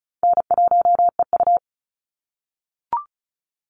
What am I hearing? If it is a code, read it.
Morse code, heard as a keyed tone, about 34 wpm, 700 Hz. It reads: N1EU
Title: NA SOTA Activator Callsigns 35wpm